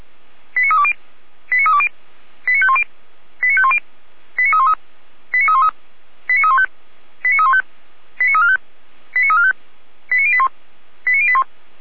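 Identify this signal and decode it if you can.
Five-tone selective call (ZVEI), heard as quick runs of five tones, tones of about 70 ms. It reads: 89320 89320 89510 89510 89313 89313 89315 89315 89345 89345 89081 89081
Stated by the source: ZVEI five-tone selective call received with an SDR, from feuerwehr-orlamuende (Orlamünde fire brigade).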